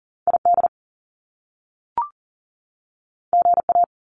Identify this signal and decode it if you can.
Morse code, heard as a keyed tone, about 40 wpm, 700 Hz. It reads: ID GA
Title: US States 40wpm